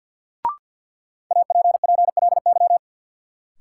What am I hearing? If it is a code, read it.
Morse code, heard as a keyed tone, about 53 wpm, 700 Hz. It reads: APPLY
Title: Sentences from Top 1200 Words - New Word Drill Alt - 50wpm